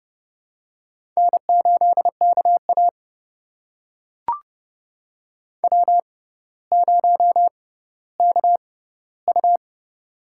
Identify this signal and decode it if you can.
Morse code, heard as a keyed tone, about 31 wpm, spaced as for 16 wpm, 700 Hz. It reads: N8KA W0KU
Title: USA Call Signs - 4 Characters - 6x Character Spacing - 30wpm